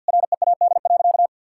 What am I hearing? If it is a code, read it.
Morse code, heard as a keyed tone, about 50 wpm, 700 Hz. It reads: READ.
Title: Sentences from Top 1000 Words - Review Part 5 - 50wpm